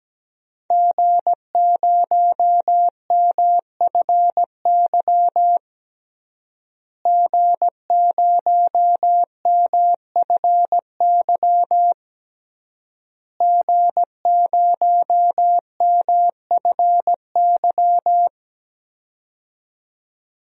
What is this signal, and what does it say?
Morse code, heard as a keyed tone, about 17 wpm, 700 Hz. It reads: G0MFY G0MFY G0MFY
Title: DXCC-100 Call Signs - 5 Characters - Repeated 3x - 17wpm